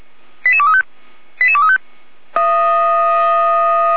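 Five-tone selective call (ZVEI), heard as quick runs of five tones, tones of about 70 ms. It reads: 88235 88235
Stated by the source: ZVEI five-tone selective call for 88288, received by SDR.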